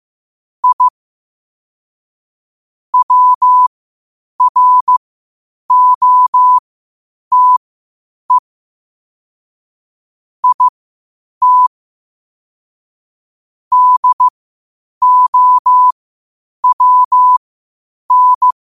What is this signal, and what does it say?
Morse code, heard as a keyed tone, about 15 wpm, spaced as for 8 wpm, 1000 Hz. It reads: I WROTE IT DOWN
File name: qotdfiles / 8wpm